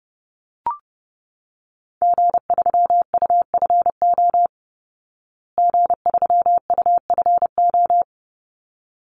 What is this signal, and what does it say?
Morse code, heard as a keyed tone, about 30 wpm, 700 Hz. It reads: G3UFO G3UFO
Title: Call Signs Repeated 3x - 30wpm